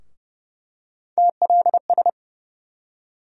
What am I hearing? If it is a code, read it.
Morse code, heard as a keyed tone, about 30 wpm, 700 Hz. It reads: TLS
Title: USA Call Sign Postfixes - 3 Letters 30wpm